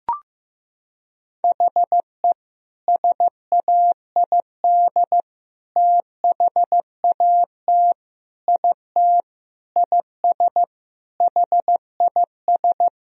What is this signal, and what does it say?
Morse code, heard as a keyed tone, about 16 wpm, 700 Hz. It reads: HE SAID THAT IT IS HIS
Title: Sentences from Top 1000 Words - Review Part 2 - 15wpm